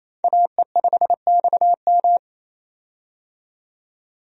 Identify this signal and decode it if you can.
Morse code, heard as a keyed tone, about 28 wpm, 700 Hz. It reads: AE5XM
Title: USA Call Signs - 5 Characters - 28wpm